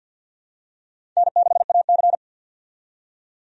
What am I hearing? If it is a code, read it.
Morse code, heard as a keyed tone, about 52 wpm, 700 Hz. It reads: N6AC
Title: USA Call Signs - 4 Characters - 50wpm